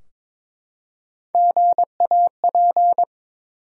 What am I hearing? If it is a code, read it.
Morse code, heard as a keyed tone, about 22 wpm, 700 Hz. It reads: GAP
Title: Three Letter Words - 22wpm